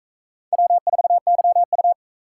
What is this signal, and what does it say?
Morse code, heard as a keyed tone, about 42 wpm, 700 Hz. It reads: W4YU